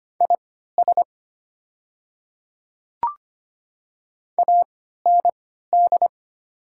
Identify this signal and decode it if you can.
Morse code, heard as a keyed tone, about 25 wpm, spaced as for 14 wpm, 700 Hz. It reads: IS AND
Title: 3x Character Spacing - Top 100 Words - 25wpm